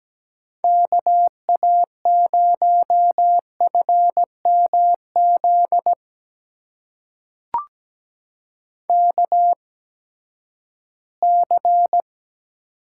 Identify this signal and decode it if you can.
Morse code, heard as a keyed tone, about 17 wpm, spaced as for 12 wpm, 700 Hz. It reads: KA0FMZ KC
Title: USA Call Signs - 6 Characters - 8x Character Spacing - 17wpm